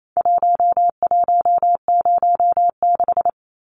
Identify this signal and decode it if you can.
Morse code, heard as a keyed tone, about 28 wpm, 700 Hz. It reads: 1106
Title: Numbers - Contest Style - 4-Digit 28wpm